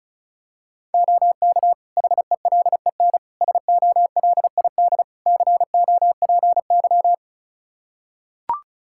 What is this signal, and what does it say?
Morse code, heard as a keyed tone, about 34 wpm, 700 Hz. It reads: OK HELEN SOLID COPY
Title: Practice QSO - Exchange 3 - More Info - 35wpm